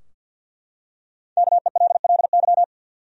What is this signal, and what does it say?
Morse code, heard as a keyed tone, about 52 wpm, 700 Hz. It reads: KELLY
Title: USA Common Names - Max 5 Letters - 50wpm